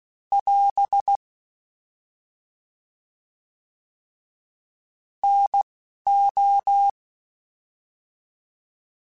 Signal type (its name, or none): Morse code